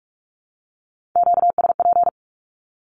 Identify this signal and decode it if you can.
Morse code, heard as a keyed tone, about 44 wpm, 700 Hz. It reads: QSP
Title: Q-Codes 45wpm